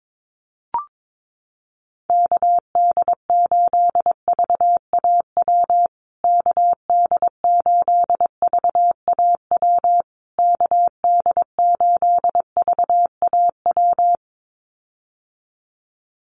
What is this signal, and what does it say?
Morse code, heard as a keyed tone, about 22 wpm, 700 Hz. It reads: KD8VAW KD8VAW KD8VAW
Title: USA Call Signs Repeated 3x 22wpm